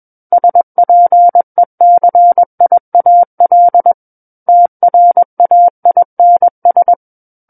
Morse code, heard as a keyed tone, about 21 wpm, 700 Hz. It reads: SPECIAL TRAINS
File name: Book One / 12-Chapter 12 - 21 WPM